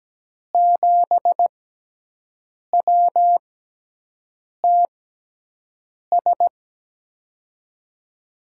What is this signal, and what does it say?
Morse code, heard as a keyed tone, about 17 wpm, spaced as for 6 wpm, 700 Hz. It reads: 7WTS
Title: USA Call Signs - 5 Characters - 6x Character Spacing - 17wpm